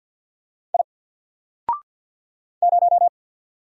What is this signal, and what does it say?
Morse code, heard as a keyed tone, about 48 wpm, 700 Hz. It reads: I 0
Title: Single Character (Letter:Number:Punctuation:Prosign) 50wpm